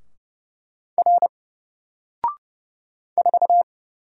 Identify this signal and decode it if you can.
Morse code, heard as a keyed tone, about 30 wpm, 700 Hz. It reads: R 4
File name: Single-Letter-Number-30wpm